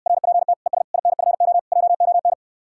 Morse code, heard as a keyed tone, about 69 wpm, 700 Hz. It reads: RPT ES INFO JON